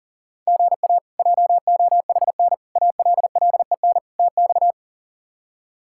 Morse code, heard as a keyed tone, about 40 wpm, 700 Hz. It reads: GA JOHN ALLEN TX
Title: K1USN SST QSO Practice - 40wpm